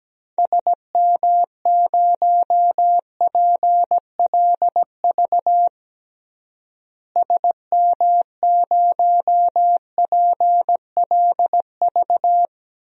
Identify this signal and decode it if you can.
Morse code, heard as a keyed tone, about 17 wpm, 700 Hz. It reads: SM0PLV SM0PLV
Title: DXCC-100 Call Signs - 6 Characters - Repeated 2x - 17wpm